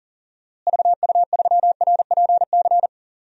Morse code, heard as a keyed tone, about 40 wpm, 700 Hz. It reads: VU3RPC
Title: DXCC-100 Call Signs - 6 Characters - Repeated 2x - 40wpm